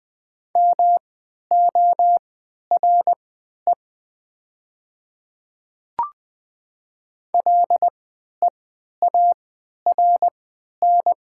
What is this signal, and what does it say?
Morse code, heard as a keyed tone, about 20 wpm, spaced as for 11 wpm, 700 Hz. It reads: MORE LEARN
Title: 3x Character Spacing - Top 300 Words - 20wpm